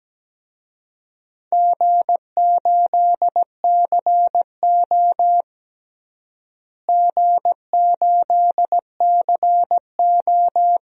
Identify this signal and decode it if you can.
Morse code, heard as a keyed tone, about 17 wpm, 700 Hz. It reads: G8CO G8CO
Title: DXCC-100 Call Signs - 4 Characters - Repeated 2x - 17wpm